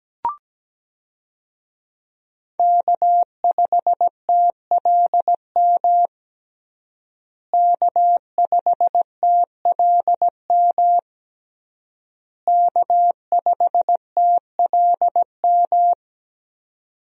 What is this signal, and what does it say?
Morse code, heard as a keyed tone, about 17 wpm, 700 Hz. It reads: K5TLM K5TLM K5TLM
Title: USA Call Signs - 5 Characters - Repeated 3x - 17wpm